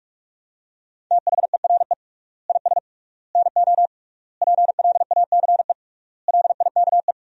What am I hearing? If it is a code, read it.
Morse code, heard as a keyed tone, about 42 wpm, 700 Hz. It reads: THERE IS NO PLACE LIKE
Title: 3x Word-Spacing - Sentences from Top 200 Words - 45wpm